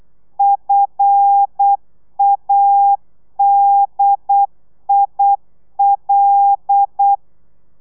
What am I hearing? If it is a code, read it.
Morse code, heard as a keyed tone, about 8 wpm, 800 Hz. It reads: FADIL
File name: telegrafia / i2tfj / Lez5 35